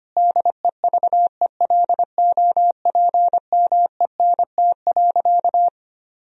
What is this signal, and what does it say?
Morse code, heard as a keyed tone, about 25 wpm, 700 Hz. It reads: DEVELOPMENT.
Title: Sentences from Top 1000 Words - Review Part 4 - 25wpm